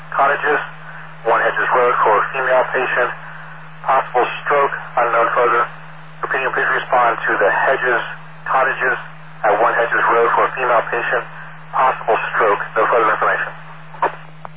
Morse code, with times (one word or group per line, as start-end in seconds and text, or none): none